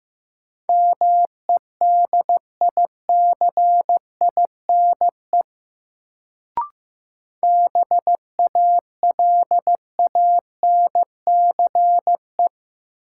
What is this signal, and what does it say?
Morse code, heard as a keyed tone, about 15 wpm, 700 Hz. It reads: MEDICINE BALANCE
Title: Sentences from Top 900 Words - New Word Drill - 15wpm